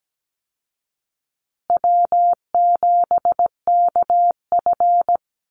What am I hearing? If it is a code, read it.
Morse code, heard as a keyed tone, about 17 wpm, 700 Hz. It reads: W7KF
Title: Easy Call Signs Repeated 3x - 17wpm